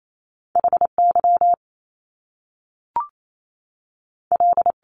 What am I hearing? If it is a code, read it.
Morse code, heard as a keyed tone, about 28 wpm, spaced as for 3 wpm, 700 Hz. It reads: H YL
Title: Top Two Letter Combinations in Words 28wpm